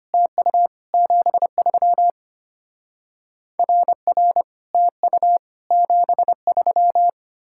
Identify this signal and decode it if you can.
Morse code, heard as a keyed tone, about 25 wpm, 700 Hz. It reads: TU 73 RR TU 73
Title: Speed-Racing Common QSO Phrases - 25wpm